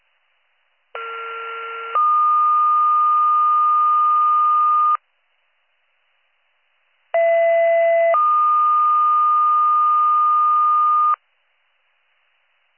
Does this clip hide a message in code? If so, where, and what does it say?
Yes: two-tone page from 1.0 to 11.2 s, heard as a one-second tone, then a three-second tone, about 1370 then 1220 Hz, about 670 then 1220 Hz.